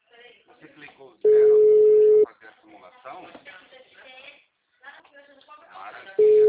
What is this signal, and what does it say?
Ringback tone (Europe, 425 Hz), with speech.
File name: out- -2011- - - .18489